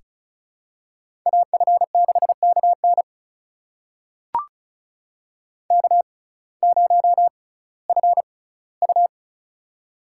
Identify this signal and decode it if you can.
Morse code, heard as a keyed tone, about 36 wpm, spaced as for 19 wpm, 700 Hz. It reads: AF6KN K0FU
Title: USA Call Signs - 6x Character Spacing - 35wpm